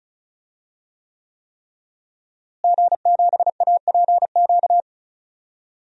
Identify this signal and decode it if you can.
Morse code, heard as a keyed tone, about 36 wpm, 700 Hz. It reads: G7APQ
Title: DXCC-100 Call Signs - 5 Characters - Repeated 3x - 35wpm